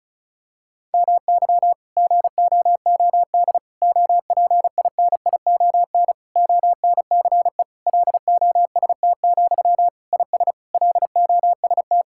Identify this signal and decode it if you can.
Morse code, heard as a keyed tone, about 36 wpm, 690 Hz. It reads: MY GOOD OPINION ONCE LOST, IS LOST
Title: Sentences from Top 1600 Words 35wpm